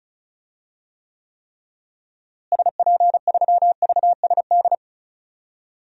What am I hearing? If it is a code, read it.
Morse code, heard as a keyed tone, about 34 wpm, 700 Hz. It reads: SP3VSD
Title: DXCC-100 Call Signs - 6 Characters - Repeated 3x - 35wpm